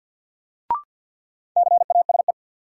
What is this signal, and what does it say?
Morse code, heard as a keyed tone, about 53 wpm, 700 Hz. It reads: CASE
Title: Sets of 2 Words - New Word Drill - 50wpm